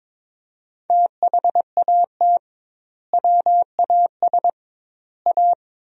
Morse code, heard as a keyed tone, about 22 wpm, 700 Hz. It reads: THAT WAS A
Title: 2x Word-Spacing - Sentences from Top 200 Words - 22wpm